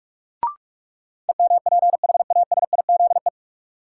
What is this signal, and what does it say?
Morse code, heard as a keyed tone, about 46 wpm, 700 Hz. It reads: EMPHASIZE
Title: Sentences from Top 1500 Words - New Word Drill Alt - 45wpm